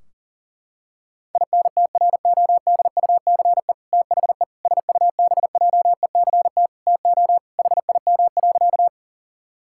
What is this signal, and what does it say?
Morse code, heard as a keyed tone, about 40 wpm, 700 Hz. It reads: INTRODUCE THE SUBJECT TO HIM.
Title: Sentences from Top 1600 Words 40wpm